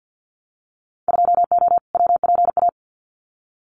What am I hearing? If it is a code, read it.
Morse code, heard as a keyed tone, about 50 wpm, 700 Hz. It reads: 2O WPA
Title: Winter Field Day Exchanges Encore 50wpm